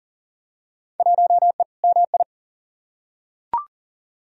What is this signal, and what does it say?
Morse code, heard as a keyed tone, about 40 wpm, 700 Hz. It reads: 1E MI
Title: Field Day Exchanges - 40wpm